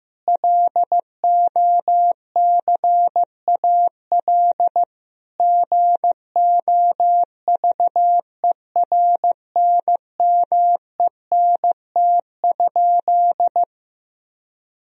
Morse code, heard as a keyed tone, about 15 wpm, 700 Hz. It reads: LOCAL GOVERNMENT?
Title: Sentences from Top 1000 Words - Review Part 4 - 15wpm